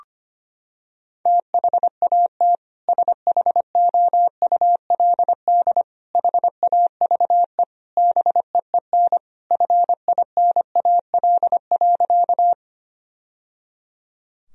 Morse code, heard as a keyed tone, about 25 wpm, 700 Hz. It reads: THAT SHOULD HAVE BEEN FINAL.
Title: Sentences from Top 1000 Words - Review Part 2 - 25wpm